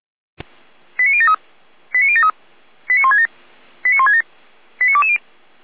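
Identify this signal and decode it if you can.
Five-tone selective call (ZVEI), heard as quick runs of five tones, tones of about 80 ms. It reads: 89073 89073 89167 89167 89220
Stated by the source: Five-tone ZVEI selective call received by SDR, from feuerwehr-orlamuende.org fire brigade archive.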